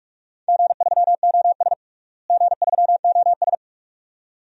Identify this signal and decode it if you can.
Morse code, heard as a keyed tone, about 46 wpm, 700 Hz. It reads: G3OS G3OS
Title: DXCC-100 Call Signs - 4 Characters - Repeated 2x - 45wpm